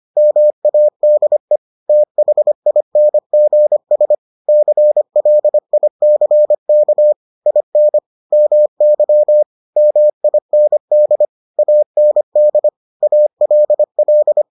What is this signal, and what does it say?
Morse code, heard as a keyed tone, about 25 wpm, 600 Hz. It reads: MADE THINGS CLICK IN MY MIND AND ALL